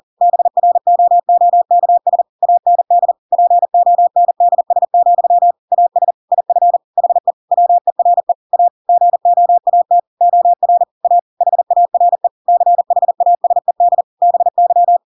Morse code, heard as a keyed tone, about 40 wpm, 700 Hz. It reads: BROOKS AND PONDS, AS IF HE WERE A GOAT OR A HARE CHASED BY
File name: Pinocchio / Pinocchio ch4 40 40 ik1hge